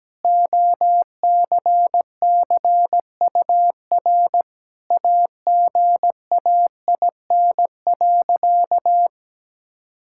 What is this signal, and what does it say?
Morse code, heard as a keyed tone, about 17 wpm, 700 Hz. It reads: OCCUR AGAIN.
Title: Sentences from Top 1000 Words - Review Part 6 - 17wpm